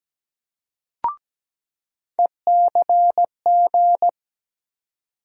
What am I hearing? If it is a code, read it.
Morse code, heard as a keyed tone, about 17 wpm, 700 Hz. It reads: ECG
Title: USA Call Sign Postfixes - 3 Letters 17wpm